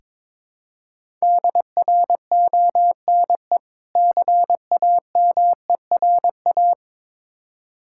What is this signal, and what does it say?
Morse code, heard as a keyed tone, about 22 wpm, 690 Hz. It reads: DRONE CAMERA
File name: Difficult Words with Hints - 22wpm